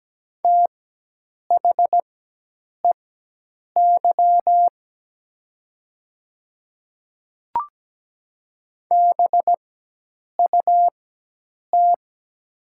Morse code, heard as a keyed tone, about 17 wpm, spaced as for 8 wpm, 700 Hz. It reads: THEY BUT
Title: 4x Character Spacing - Top 100 Words - 17wpm